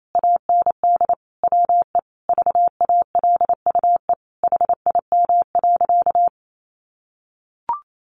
Morse code, heard as a keyed tone, about 28 wpm, 700 Hz. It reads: AND WE VALUE HIM.
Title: Sentences from Top 600 Words 28wpm